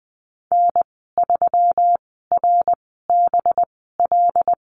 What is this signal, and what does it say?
Morse code, heard as a keyed tone, about 20 wpm, spaced as for 14 wpm, 700 Hz. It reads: N3RBL
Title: 193 Lesson 31g - Letter v - Call Signs 20wpm